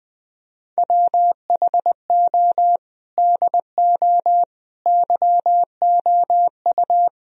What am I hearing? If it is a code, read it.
Morse code, heard as a keyed tone, about 20 wpm, 700 Hz. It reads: WHO DO YOU
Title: Sentences from Top 1000 Words - Review Part 4 - 20wpm